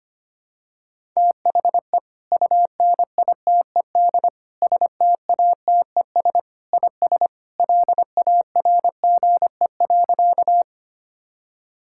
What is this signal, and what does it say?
Morse code, heard as a keyed tone, about 25 wpm, 700 Hz. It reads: THE UNITED STATES IS LARGE.
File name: practice / 25wpm / Sentences with 3-Word Phrases - 25wpm